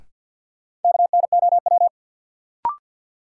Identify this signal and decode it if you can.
Morse code, heard as a keyed tone, about 53 wpm, 700 Hz. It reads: KNOW
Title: Sentences from Top 100 Words - New Word Drill - 50wpm